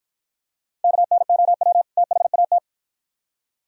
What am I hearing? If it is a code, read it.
Morse code, heard as a keyed tone, about 51 wpm, 700 Hz. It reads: KNOW THAT